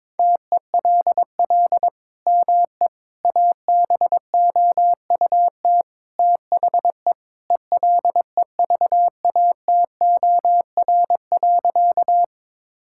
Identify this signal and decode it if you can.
Morse code, heard as a keyed tone, about 21 wpm, 700 Hz. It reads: TELL ME ABOUT THE ELEVATOR.
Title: Sentences from Top 1600 Words 22wpm